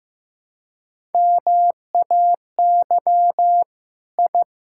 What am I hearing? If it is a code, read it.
Morse code, heard as a keyed tone, about 15 wpm, 700 Hz. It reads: MAY I
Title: Sentences from Top 1000 Words - Review Part 5 - 15wpm